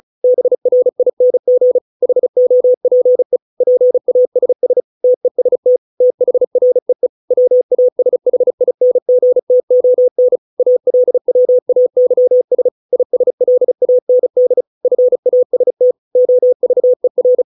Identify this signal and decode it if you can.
Morse code, heard as a keyed tone, about 34 wpm, 500 Hz. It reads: BRING HOPE PASS TEST THREE WASHINGTON ALWAYS ISLAND FAST OVER